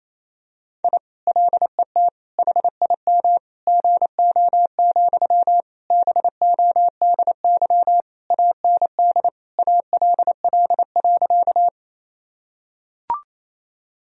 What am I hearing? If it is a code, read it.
Morse code, heard as a keyed tone, about 28 wpm, 700 Hz. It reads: I LET HIM GO, BODY AND ALL.
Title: Sentences from Top 1000 Words - Review Part 2 - 28wpm